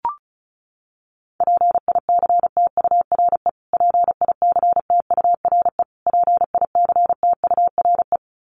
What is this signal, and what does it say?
Morse code, heard as a keyed tone, about 34 wpm, 700 Hz. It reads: PICTURE PICTURE PICTURE
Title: Top 300 Words Repeated 35wpm